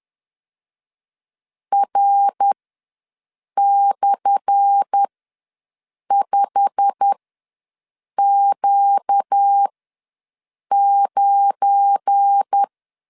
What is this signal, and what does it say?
Morse code, heard as a keyed tone, about 10 wpm, spaced as for 6 wpm, 800 Hz. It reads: R/5Q9
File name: LEZIONI / 10 SIGNS / 10D COMA ws 10wpm 4314